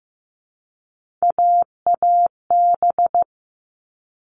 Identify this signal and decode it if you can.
Morse code, heard as a keyed tone, about 15 wpm, 700 Hz. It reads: AAB
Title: All Cut Numbers - 3-Digit 15wpm